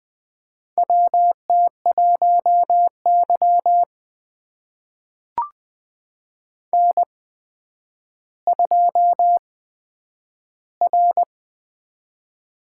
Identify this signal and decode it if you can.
Morse code, heard as a keyed tone, about 20 wpm, spaced as for 10 wpm, 700 Hz. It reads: WT1Y N2R
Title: USA Call Signs - 4 Characters - 8x Character Spacing - 20wpm